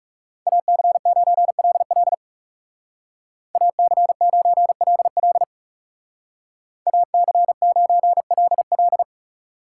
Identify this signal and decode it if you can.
Morse code, heard as a keyed tone, about 36 wpm, 700 Hz. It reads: AC9LL AC9LL AC9LL
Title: Speed Racing - USA 2x2 Call Signs 30wpm